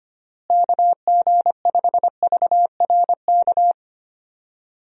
Morse code, heard as a keyed tone, about 25 wpm, 700 Hz. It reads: KG5VRK
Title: USA Call Signs - 6 Characters - Repeated 2x - 25wpm